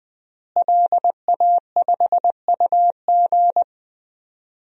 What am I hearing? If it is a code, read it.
Morse code, heard as a keyed tone, about 20 wpm, 700 Hz. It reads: LA5UG
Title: DXCC-100 Call Signs - 5 Characters - Repeated 2x - 20wpm